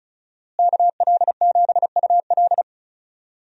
Morse code, heard as a keyed tone, about 36 wpm, 700 Hz. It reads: KL7UL
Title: USA Call Signs - 5 Characters - Repeated 3x - 35wpm